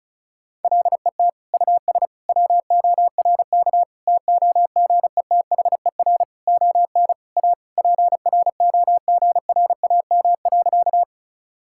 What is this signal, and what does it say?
Morse code, heard as a keyed tone, about 36 wpm, 700 Hz. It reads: LET US WORK TOGETHER ON A PROGRAM.
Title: Sentences from Top 1000 Words - Review Part 5 - 35wpm